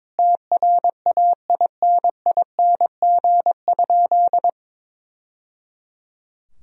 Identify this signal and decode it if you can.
Morse code, heard as a keyed tone, about 22 wpm, 700 Hz. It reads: TRAINING?
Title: Sentences from Top 1000 Words - Review Part 4 - 22wpm